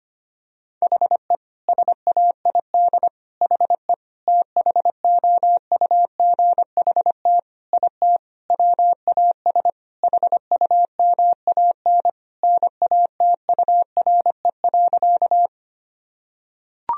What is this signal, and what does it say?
Morse code, heard as a keyed tone, about 25 wpm, 700 Hz. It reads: HE SAID HE THOUGHT IT WAS HUMAN NATURE.
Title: Sentences from Top 1000 Words - Review Part 3 - 25wpm